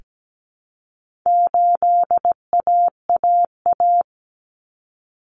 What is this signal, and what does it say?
Morse code, heard as a keyed tone, about 17 wpm, 700 Hz. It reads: 8AAA
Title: Common Cut Numbers - 4-Digit 17wpm